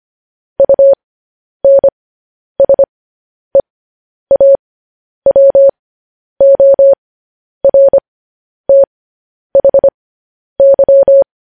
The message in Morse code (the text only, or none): UNSEAWORTHY